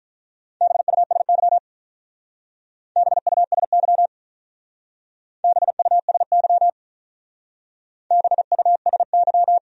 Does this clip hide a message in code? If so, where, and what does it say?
Yes: Morse code from 0.6 to 9.6 s, heard as a keyed tone, about 45 wpm, 700 Hz. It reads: BUSY BUSY BUSY BUSY